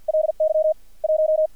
Morse code, heard as a keyed tone, about 50 wpm, 630 Hz. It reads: LY 1
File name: archived / CRSnet